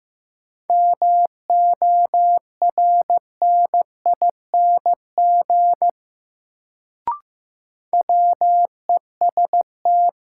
Morse code, heard as a keyed tone, about 15 wpm, 700 Hz. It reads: MORNING WEST
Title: Sentences from Top 400 Words - New Word Drill - 15wpm